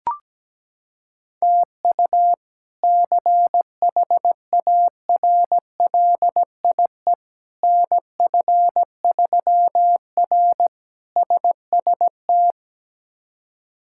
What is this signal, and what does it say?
Morse code, heard as a keyed tone, about 17 wpm, 700 Hz. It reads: TU CHARLIE NF3R SST
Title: K1USN SST QSO Practice - 17wpm